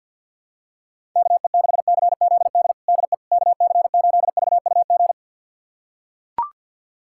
Morse code, heard as a keyed tone, about 45 wpm, 700 Hz. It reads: KE6CZD DE KC8VUG
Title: USA Callsign DE Callsign - 6 Characters - 50wpm